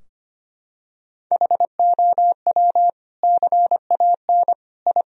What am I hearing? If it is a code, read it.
Morse code, heard as a keyed tone, about 25 wpm, 700 Hz. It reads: HOW CAN I
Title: Sentences from Top 1000 Words - Review Part 3 - 25wpm